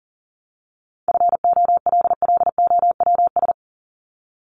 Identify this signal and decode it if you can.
Morse code, heard as a keyed tone, about 40 wpm, 700 Hz. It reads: FOLLOWS
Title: ES:S Suffix - Top 100 Words with Base Words 40wpm